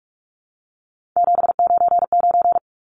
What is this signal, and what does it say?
Morse code, heard as a keyed tone, about 44 wpm, 700 Hz. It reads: 799
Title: Numbers - 3-Digit 45wpm